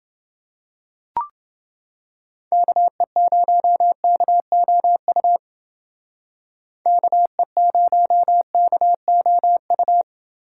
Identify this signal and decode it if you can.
Morse code, heard as a keyed tone, about 28 wpm, 700 Hz. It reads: KE0KOU KE0KOU